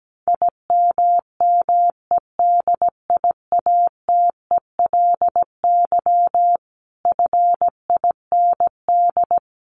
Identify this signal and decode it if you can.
Morse code, heard as a keyed tone, about 17 wpm, 700 Hz. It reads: IMMEDIATELY FIND
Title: Sentences from Top 1200 Words 17wpm